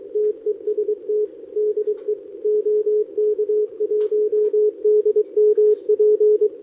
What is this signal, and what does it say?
Morse code, heard as a keyed tone, about 23 wpm, 410 Hz, with radio noise. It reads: TEST DE OK1DMP
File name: OK QRP 2026 / OKQRP2026